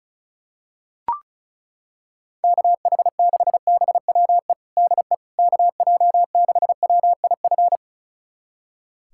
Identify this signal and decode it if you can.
Morse code, heard as a keyed tone, about 36 wpm, 700 Hz. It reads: KH6BWE DE KJ6WIF
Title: USA Callsign DE Callsign 35wpm